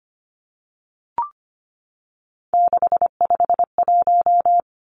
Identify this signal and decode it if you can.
Morse code, heard as a keyed tone, about 25 wpm, 690 Hz. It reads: 651
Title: Numbers - 3-Digit 25wpm